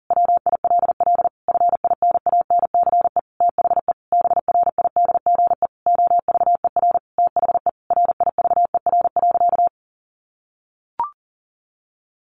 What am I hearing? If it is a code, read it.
Morse code, heard as a keyed tone, about 40 wpm, 700 Hz. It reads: WILL FINANCE THE BRIDGE OVER THE RIVER.
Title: Sentences from Top 1500 Words 40wpm